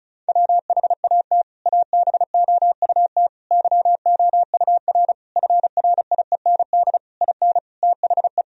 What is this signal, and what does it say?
Morse code, heard as a keyed tone, about 34 wpm, 700 Hz. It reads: WHAT ABOUT YOUR FRIEND IN THE
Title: Sentences from Top 1000 Words - Review Part 1 - 35wpm